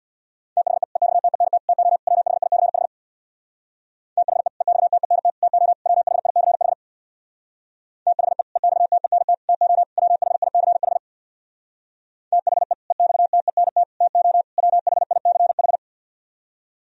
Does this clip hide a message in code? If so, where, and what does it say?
Yes: Morse code from 0.6 to 15.8 s, heard as a keyed tone, about 66 wpm, 690 Hz. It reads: THE EXTENT TO WHICH THE EXTENT TO WHICH THE EXTENT TO WHICH THE EXTENT TO WHICH